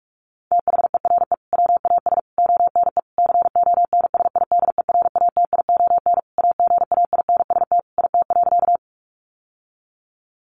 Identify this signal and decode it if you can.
Morse code, heard as a keyed tone, about 44 wpm, 700 Hz. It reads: THERE WAS ONE CONSIDERATION AGAINST IT.
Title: Sentences from Top 1500 Words 45wpm